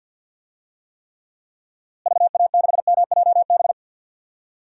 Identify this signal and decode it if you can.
Morse code, heard as a keyed tone, about 52 wpm, 700 Hz. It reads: VA6MJB
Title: DXCC-100 Call Signs - 6 Characters - Repeated 2x - 50wpm